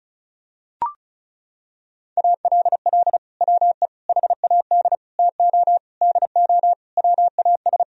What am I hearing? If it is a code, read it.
Morse code, heard as a keyed tone, about 36 wpm, 700 Hz. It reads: ALL WE HAD TO DO WAS
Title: Sentences from Top 1000 Words - Review Part 4 - 35wpm